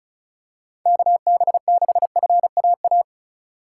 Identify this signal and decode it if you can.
Morse code, heard as a keyed tone, about 36 wpm, 700 Hz. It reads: KB6FAA
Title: USA Call Signs - 6 Characters - 4x Character Spacing - 35wpm